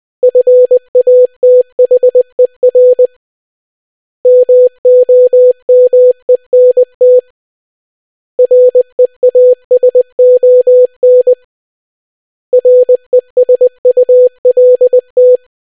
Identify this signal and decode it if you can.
Morse code, heard as a keyed tone, about 20 wpm, 500 Hz. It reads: FATHER MOMENT REASON RESULT